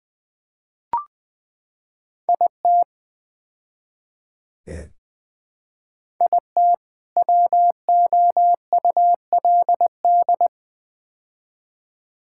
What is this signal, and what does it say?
Morse code, heard as a keyed tone, about 20 wpm, 700 Hz. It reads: IT IT WOULD
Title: Build A Phrase - 4-Word Phrases - 20wpm